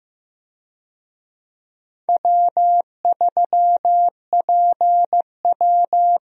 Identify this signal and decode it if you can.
Morse code, heard as a keyed tone, about 15 wpm, 700 Hz. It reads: W3PW
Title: USA Call Signs - 4 Characters - Repeated 3x - 15wpm